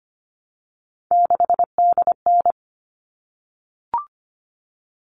Morse code, heard as a keyed tone, about 25 wpm, 700 Hz. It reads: 6DN
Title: All Cut Numbers - 3-Digit 25wpm